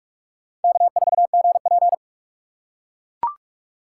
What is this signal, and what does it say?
Morse code, heard as a keyed tone, about 42 wpm, 700 Hz. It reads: K4GP
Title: USA Call Signs - 4 Characters - 2x Character Spacing - 45wpm